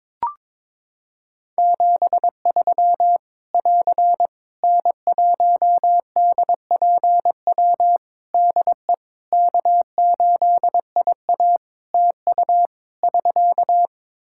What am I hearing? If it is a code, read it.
Morse code, heard as a keyed tone, about 22 wpm, 700 Hz. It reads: 73 <AR> N1DPW DE K8IA TU <SK>
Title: K1USN SST QSO Practice 22wpm